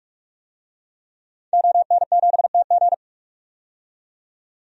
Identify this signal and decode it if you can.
Morse code, heard as a keyed tone, about 45 wpm, 700 Hz. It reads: ON7TG
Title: DXCC-100 Call Signs - 5 Characters - 45wpm